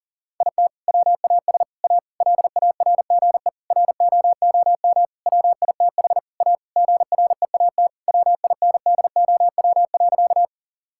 Morse code, heard as a keyed tone, about 40 wpm, 700 Hz. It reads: IT WAS A LARGE ROOM WITH A GREAT WINDOW.
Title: Sentences from Top 1600 Words 40wpm